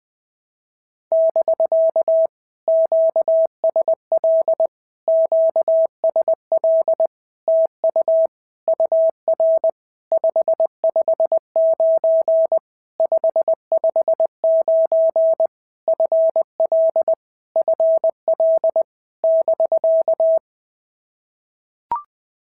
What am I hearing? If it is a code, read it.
Morse code, heard as a keyed tone, about 20 wpm, 650 Hz. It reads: <BK> QSL QSL TU UR 559 559 FL FL <BK>
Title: POTA QSO Practice - 20wpm